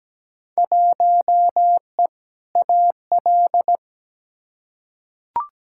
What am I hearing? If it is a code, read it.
Morse code, heard as a keyed tone, about 17 wpm, 700 Hz. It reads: 1E AL
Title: Field Day Exchanges - 17wpm